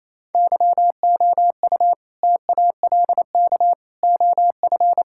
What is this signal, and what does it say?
Morse code, heard as a keyed tone, about 28 wpm, 700 Hz. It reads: YOU TALK OF
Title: Sentences from Top 1600 Words 28wpm